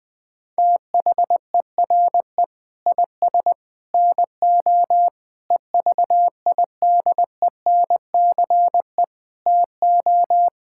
Morse code, heard as a keyed tone, about 20 wpm, 700 Hz. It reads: THERE IS NO EVIDENCE TO
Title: Sentences with 7-Word Phrases - Spoken Beginning - 20wpm